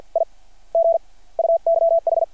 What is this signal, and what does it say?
Morse code, heard as a keyed tone, about 52 wpm, spaced as for 27 wpm, 660 Hz. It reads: IG UYH
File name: archived / CRSnet